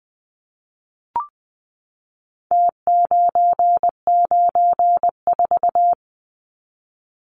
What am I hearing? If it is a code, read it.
Morse code, heard as a keyed tone, about 20 wpm, 700 Hz. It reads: T994
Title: Numbers - Contest Style - 4-Digit 20wpm